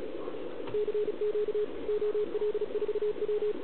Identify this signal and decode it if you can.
Morse code, heard as a keyed tone, about 37 wpm, 420 Hz, with radio noise. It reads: CQ OL4W
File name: truSDX/KX3a